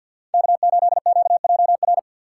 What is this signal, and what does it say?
Morse code, heard as a keyed tone, about 49 wpm, 700 Hz. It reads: K8QJR
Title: USA Call Signs - 5 Characters - 4x Character Spacing - 50wpm